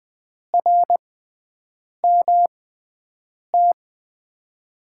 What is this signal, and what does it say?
Morse code, heard as a keyed tone, about 20 wpm, spaced as for 7 wpm, 700 Hz. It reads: RMT
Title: USA Call Signs - 6x Character Spacing - 20wpm